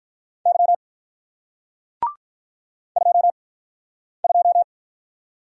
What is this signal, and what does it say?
Morse code, heard as a keyed tone, about 50 wpm, spaced as for 9 wpm, 700 Hz. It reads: Y 22